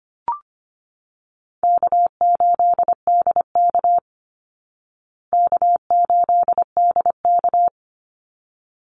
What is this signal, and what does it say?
Morse code, heard as a keyed tone, about 25 wpm, 700 Hz. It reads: K8DK K8DK
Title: Easy Call Signs Repeated 2x - 25wpm